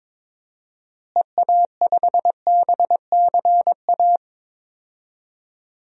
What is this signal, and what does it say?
Morse code, heard as a keyed tone, about 21 wpm, 700 Hz. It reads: EA5BCA
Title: DXCC-100 Call Signs - 6 Characters - Repeated 2x - 22wpm